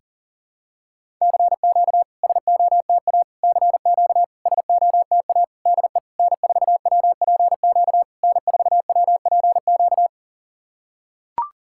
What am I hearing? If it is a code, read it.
Morse code, heard as a keyed tone, about 40 wpm, 700 Hz. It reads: CQ SOTA CQ SOTA DE N4WPQ N4WPQ
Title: SOTA QSO Practice 40wpm